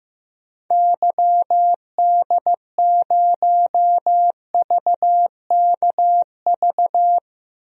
Morse code, heard as a keyed tone, about 15 wpm, 700 Hz. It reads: YD0VKV
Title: DXCC-100 Call Signs - 6 Characters - 15wpm